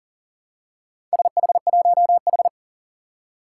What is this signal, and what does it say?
Morse code, heard as a keyed tone, about 40 wpm, 700 Hz. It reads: SH1H
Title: Four Challenging Letter-Number Combinations 40wpm